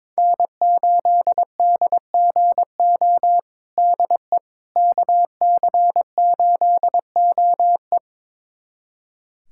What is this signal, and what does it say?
Morse code, heard as a keyed tone, about 22 wpm, 700 Hz. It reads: N8DGO DE KC8OE
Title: USA Callsign DE Callsign - 5 Characters - 22wpm